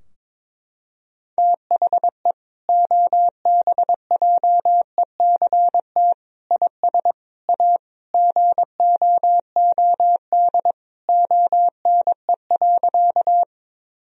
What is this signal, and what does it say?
Morse code, heard as a keyed tone, about 22 wpm, 700 Hz. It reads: THE OBJECT IS A GOOD ONE.
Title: Sentences from Top 1000 Words - Review Part 3 - 22wpm